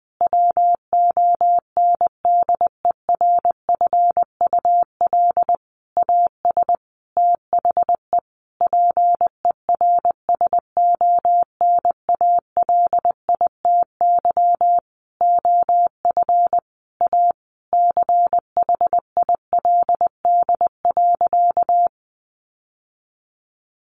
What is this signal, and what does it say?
Morse code, heard as a keyed tone, about 20 wpm, 700 Hz. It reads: WONDERFUL AS THE PERSONALITY OF A CHILD.
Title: Sentences from Top 1400 Words 20wpm